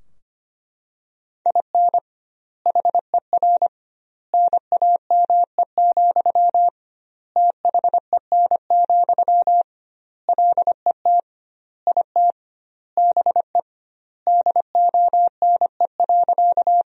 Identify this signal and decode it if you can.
Morse code, heard as a keyed tone, about 25 wpm, 700 Hz. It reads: IN HER NAME, THEN, LET IT BE DONE.
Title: 2x Word-Spacing - Sentences from Top 500 Words - 25wpm